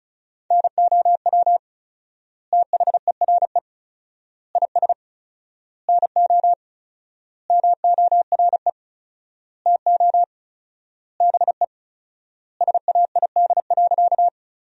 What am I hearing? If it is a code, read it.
Morse code, heard as a keyed tone, about 34 wpm, 700 Hz. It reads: NOW THERE IS NO MORE TO BE SAID.
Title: 4x Word-Spacing - Sentences from Top 100 Words - 35wpm